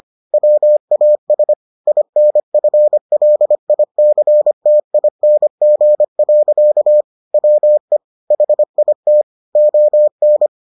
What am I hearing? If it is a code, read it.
Morse code, heard as a keyed tone, about 25 wpm, 600 Hz. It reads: WAS INFLICTING. WE HIT ON